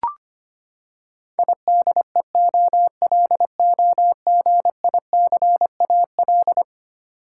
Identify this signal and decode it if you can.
Morse code, heard as a keyed tone, about 25 wpm, 700 Hz. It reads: IDEOLOGICAL
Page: morse-code-ninja-practice-25wpm